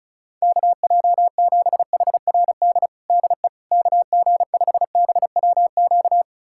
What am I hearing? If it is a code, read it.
Morse code, heard as a keyed tone, about 34 wpm, 700 Hz. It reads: KJ7HRD DE KG5BWQ
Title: USA Callsign DE Callsign 35wpm